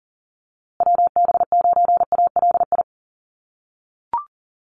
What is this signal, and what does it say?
Morse code, heard as a keyed tone, about 40 wpm, 700 Hz. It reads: WB9ALI
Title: USA Call Signs Repeated 3x 40wpm